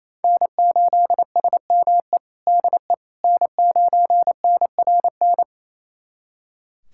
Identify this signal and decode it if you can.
Morse code, heard as a keyed tone, about 28 wpm, 700 Hz. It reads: N8SME DE N9NRN
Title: USA Callsign DE Callsign - 5 Characters - 28wpm